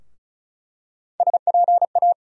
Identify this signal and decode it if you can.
Morse code, heard as a keyed tone, about 34 wpm, 700 Hz. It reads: SPA
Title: USA Call Sign Postfixes - 3 Letters 35wpm